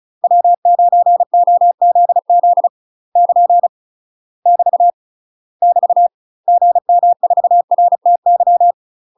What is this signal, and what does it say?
Morse code, heard as a keyed tone, about 34 wpm, 700 Hz. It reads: W9OZZ <KN> <BT> <BT> GM4RTY
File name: UK MASTER QSOs TRAINING / 35WPM / UK MASTER QSOs 0025